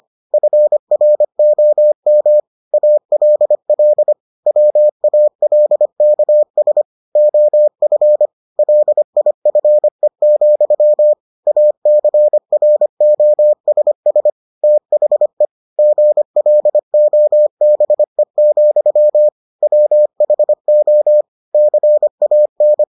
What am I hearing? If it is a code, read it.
Morse code, heard as a keyed tone, about 25 wpm, 600 Hz. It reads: FROM ALL WALKS OF LIFE, ACROSS THE GLOBE, WHO CAN